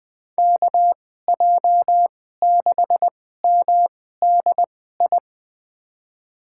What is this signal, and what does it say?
Morse code, heard as a keyed tone, about 20 wpm, spaced as for 14 wpm, 700 Hz. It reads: KJ6MDI
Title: USA Call Signs - 6 Characters - 2x Character Spacing - 20wpm